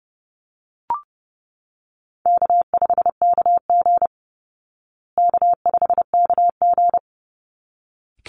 Morse code, heard as a keyed tone, about 29 wpm, 700 Hz. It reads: K5KG K5KG
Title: Easy Call Signs Repeated 2x - 30wpm